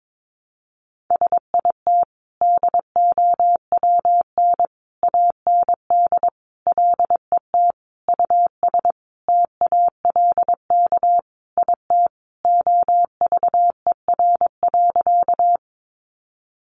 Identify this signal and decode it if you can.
Morse code, heard as a keyed tone, about 22 wpm, 700 Hz. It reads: SIT DOWN AND LET US TALK IT OVER.
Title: Sentences from Top 500 Words 22wpm